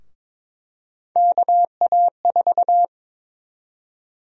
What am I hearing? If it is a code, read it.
Morse code, heard as a keyed tone, about 22 wpm, 700 Hz. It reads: KA4
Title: USA Call Sign Prefixes - Repeated 2x - 22wpm